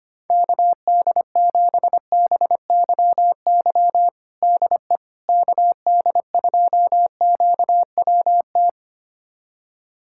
Morse code, heard as a keyed tone, about 25 wpm, 700 Hz. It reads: KD7BYY DE KD2QWT
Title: USA Callsign DE Callsign - 6 Characters - 25wpm